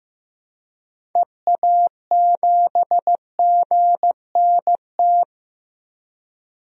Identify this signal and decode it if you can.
Morse code, heard as a keyed tone, about 15 wpm, 700 Hz. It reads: EA7GNT
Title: DXCC-100 Call Signs - 6 Characters - Repeated 3x - 15wpm